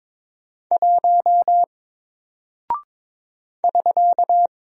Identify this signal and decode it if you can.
Morse code, heard as a keyed tone, about 22 wpm, 700 Hz. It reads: 1 <SK>
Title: Single Character (Letter:Number:Punctuation:Prosign) Alt 22wpm